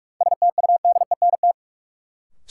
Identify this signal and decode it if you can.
Morse code, heard as a keyed tone, about 46 wpm, 700 Hz. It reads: STUDENT
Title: Sentences from Top 1000 Words - New Word Drill - 45wpm